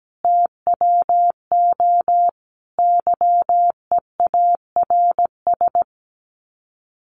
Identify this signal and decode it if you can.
Morse code, heard as a keyed tone, about 17 wpm, 700 Hz. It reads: TWO YEARS
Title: Sets of 5 Words Encore 17wpm